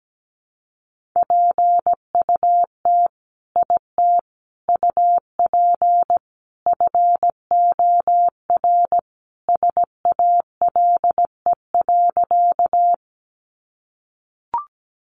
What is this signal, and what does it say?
Morse code, heard as a keyed tone, about 17 wpm, 700 Hz. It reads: PUT IT UP FOR SALE.
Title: Sentences from Top 1400 Words 17wpm